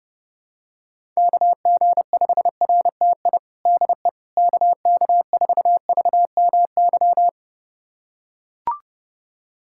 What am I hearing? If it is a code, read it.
Morse code, heard as a keyed tone, about 30 wpm, 700 Hz. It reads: KG5RTI DE KK4VMY
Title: USA Callsign DE Callsign 30wpm